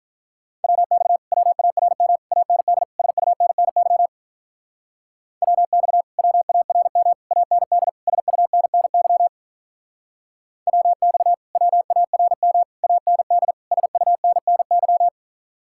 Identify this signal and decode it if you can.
Morse code, heard as a keyed tone, about 45 wpm, 700 Hz. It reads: WX WARM AND SUNNY WX WARM AND SUNNY WX WARM AND SUNNY